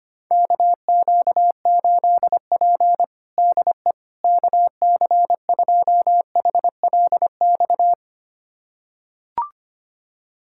Morse code, heard as a keyed tone, about 25 wpm, 700 Hz. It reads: KQ8P DE KC2HLX
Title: USA Callsign DE Callsign 25wpm